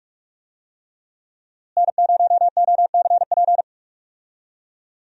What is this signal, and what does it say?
Morse code, heard as a keyed tone, about 46 wpm, 700 Hz. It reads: N0OCP
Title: USA Call Signs - 5 Characters - Repeated 3x - 45wpm